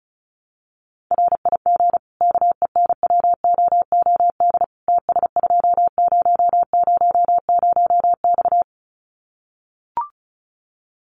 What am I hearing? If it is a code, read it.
Morse code, heard as a keyed tone, about 34 wpm, 700 Hz. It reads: RIG KENWOOD TS2000X
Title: Common QSO Phrases 35wpm